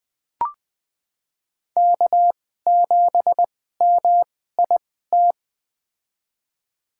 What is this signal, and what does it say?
Morse code, heard as a keyed tone, about 20 wpm, spaced as for 14 wpm, 700 Hz. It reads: K7MIT
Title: USA Call Signs - 5 Characters - 2x Character Spacing - 20wpm